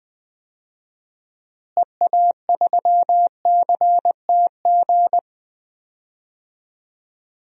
Morse code, heard as a keyed tone, about 20 wpm, 700 Hz. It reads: EA3CTG
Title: DXCC-100 Call Signs - 6 Characters - Repeated 3x - 20wpm